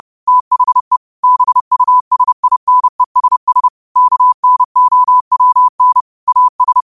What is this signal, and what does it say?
Morse code, heard as a keyed tone, about 30 wpm, 1000 Hz. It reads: THE BUSINESS KNOWN AS
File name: qotdfiles / 30wpm